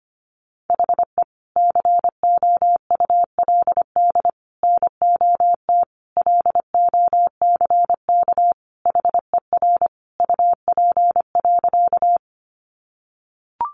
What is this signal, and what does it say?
Morse code, heard as a keyed tone, about 25 wpm, 700 Hz. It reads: HE COULD NOT LOCK HER UP.
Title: Sentences from Top 1300 Words 25wpm